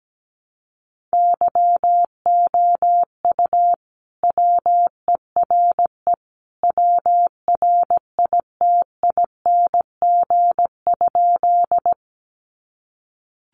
Morse code, heard as a keyed tone, about 17 wpm, 700 Hz. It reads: YOU WERE WRITING?
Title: Sentences from Top 800 Words 17wpm